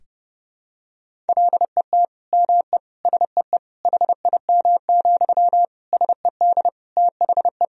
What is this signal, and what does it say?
Morse code, heard as a keyed tone, about 30 wpm, 700 Hz. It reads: LET ME SEE HIM, SED THE
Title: Sentences from Top 400 Words and Abbreviations 30wpm